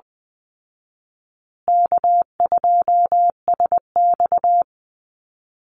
Morse code, heard as a keyed tone, about 20 wpm, 700 Hz. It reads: K2SX
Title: Call Signs Repeated 3x - 20wpm